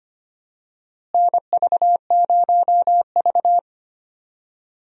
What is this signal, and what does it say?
Morse code, heard as a keyed tone, about 25 wpm, 700 Hz. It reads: NV0V
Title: USA Call Signs - 4 Characters - Repeated 2x - 25wpm